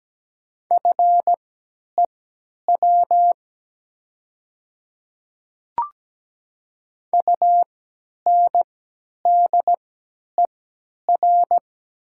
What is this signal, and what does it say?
Morse code, heard as a keyed tone, about 17 wpm, spaced as for 10 wpm, 700 Hz. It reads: FEW UNDER
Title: 3x Character Spacing - Top 300 Words - 17wpm